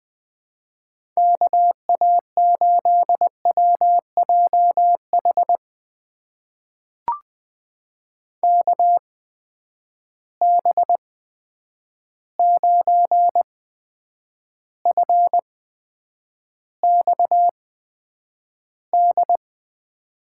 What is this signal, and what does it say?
Morse code, heard as a keyed tone, about 20 wpm, spaced as for 9 wpm, 700 Hz. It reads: KA8WJH KB9FXD
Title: USA Call Signs - 6 Characters - 8x Character Spacing - 20wpm